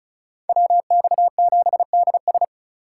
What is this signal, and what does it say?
Morse code, heard as a keyed tone, about 44 wpm, 700 Hz. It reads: WX7DS